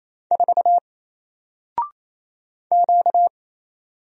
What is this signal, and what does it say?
Morse code, heard as a keyed tone, about 27 wpm, 700 Hz. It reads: 4 Q
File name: Single-Letter-Number-28wpm